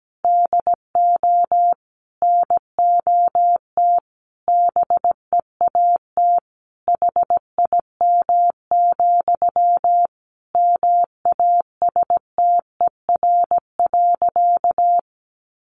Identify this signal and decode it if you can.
Morse code, heard as a keyed tone, about 17 wpm, 700 Hz. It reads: DO NOT BEAT HIM, MASTER.
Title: Sentences from Top 1300 Words 17wpm